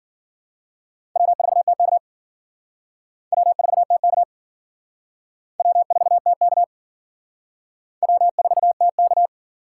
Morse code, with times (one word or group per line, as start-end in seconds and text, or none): 1.2-2.0 W4TK
3.3-4.2 W4TK
5.6-6.6 W4TK
8.0-9.3 W4TK